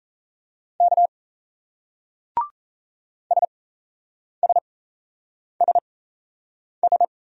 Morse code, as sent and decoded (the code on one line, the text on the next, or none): -.- / ... ... ... ...
K SSSS